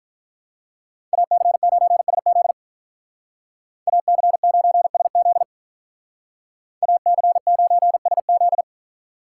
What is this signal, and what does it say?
Morse code, heard as a keyed tone, about 46 wpm, 700 Hz. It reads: AC9SZ AC9SZ AC9SZ